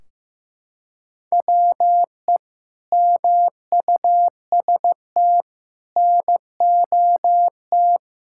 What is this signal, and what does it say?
Morse code, heard as a keyed tone, about 16 wpm, 700 Hz. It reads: WE MUST NOT
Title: Sentences from Top 1600 Words 15wpm